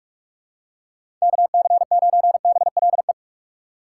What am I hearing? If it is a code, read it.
Morse code, heard as a keyed tone, about 46 wpm, 700 Hz. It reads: KC9BLE
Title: USA Call Signs - 6 Characters - 3x Character Spacing - 45wpm